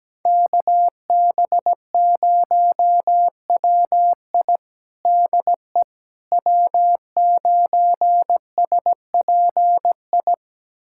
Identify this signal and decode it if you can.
Morse code, heard as a keyed tone, about 17 wpm, 700 Hz. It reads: KB0WI DE W9SPI
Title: USA Callsign DE Callsign - 5 Characters - 17wpm